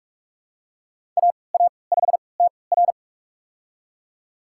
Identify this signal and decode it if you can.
Morse code, heard as a keyed tone, about 44 wpm, spaced as for 25 wpm, 700 Hz. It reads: AA5TR
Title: USA Call Signs - 5 Characters - 3x Character Spacing - 45wpm